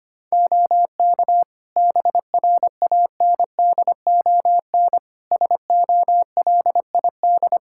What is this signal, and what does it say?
Morse code, heard as a keyed tone, about 25 wpm, 700 Hz. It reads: OK BRANDON SOLID
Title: K1USN SST QSO Practice 25wpm